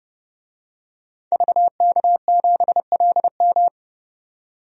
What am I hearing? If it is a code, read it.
Morse code, heard as a keyed tone, about 32 wpm, 700 Hz. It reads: VK7LM
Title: DXCC-100 Call Signs - 5 Characters - 30wpm